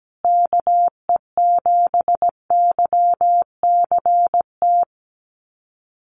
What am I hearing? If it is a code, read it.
Morse code, heard as a keyed tone, about 17 wpm, 700 Hz. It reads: KE7YCT
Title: USA Call Signs Repeated 2x 17wpm